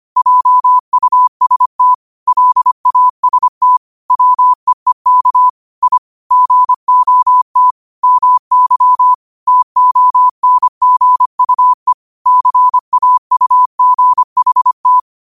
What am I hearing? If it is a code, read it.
Morse code, heard as a keyed tone, about 25 wpm, 1000 Hz. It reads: JUST LAST WEEK I GOT MY TONGUE CAUGHT
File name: 25wpm